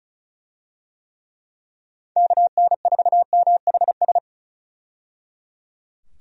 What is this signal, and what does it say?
Morse code, heard as a keyed tone, about 36 wpm, 700 Hz. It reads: KN4MHS
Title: USA Call Signs - 6 Characters - 35wpm